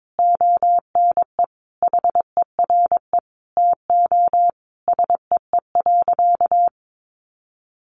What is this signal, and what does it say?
Morse code, heard as a keyed tone, about 22 wpm, 700 Hz. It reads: ONE HERE TO SEE.
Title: Sentences from Top 200 Words 22wpm